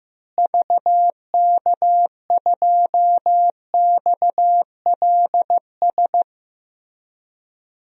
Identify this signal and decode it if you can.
Morse code, heard as a keyed tone, about 15 wpm, 700 Hz. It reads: VK2XLS
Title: DXCC-100 Call Signs - 6 Characters - Repeated 3x - 15wpm